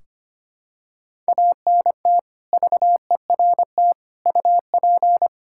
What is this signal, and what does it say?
Morse code, heard as a keyed tone, about 25 wpm, 700 Hz. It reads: ANT VERT UP
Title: Practice QSO - Exchange 2 - Info - 25wpm